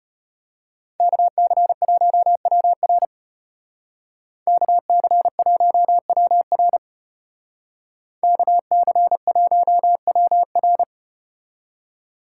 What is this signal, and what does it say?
Morse code, heard as a keyed tone, about 33 wpm, 700 Hz. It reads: KC1WR KC1WR KC1WR